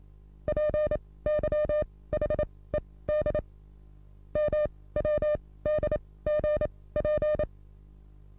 Morse code, heard as a keyed tone, about 27 wpm, spaced as for 18 wpm, 600 Hz. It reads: PYHED MWDGP
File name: repos / Morsekurs / DJ4UF Morsen27